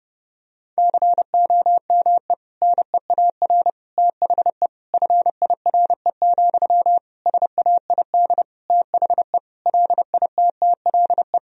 Morse code, heard as a keyed tone, about 32 wpm, 700 Hz. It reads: COME NEAR THE FIRE, SAID THE LITTLE
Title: Sentences from Top 1000 Words - Review Part 1 - 30wpm